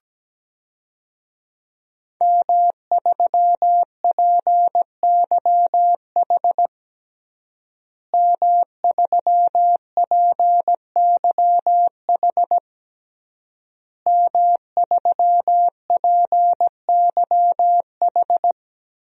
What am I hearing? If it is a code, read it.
Morse code, heard as a keyed tone, about 17 wpm, 700 Hz. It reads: M3PYH M3PYH M3PYH
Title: DXCC-100 Call Signs - 5 Characters - Repeated 3x - 17wpm